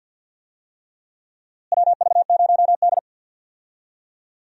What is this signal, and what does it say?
Morse code, heard as a keyed tone, about 50 wpm, 700 Hz. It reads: WV0D
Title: USA Call Signs - 4 Characters - Repeated 3x - 50wpm